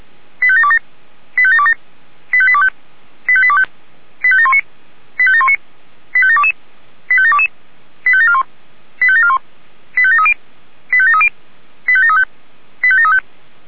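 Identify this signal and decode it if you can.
Five-tone selective call (ZVEI), heard as quick runs of five tones, tones of about 70 ms. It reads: 85727 85727 85825 85825 85719 85719 85722 85722 85741 85741 85730 85730 85735 85735
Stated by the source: ZVEI five-tone call 85745 received by SDR.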